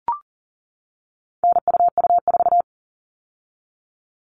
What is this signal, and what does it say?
Morse code, heard as a keyed tone, about 40 wpm, 700 Hz. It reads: NUU4
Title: All Cut Numbers - 4-Digit 40wpm